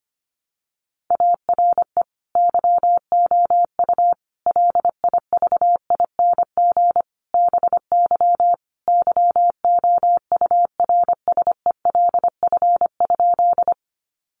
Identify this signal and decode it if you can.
Morse code, heard as a keyed tone, about 25 wpm, 700 Hz. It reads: ARE YOU LIVING BY YOURSELF?
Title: Sentences from Top 1100 Words 25wpm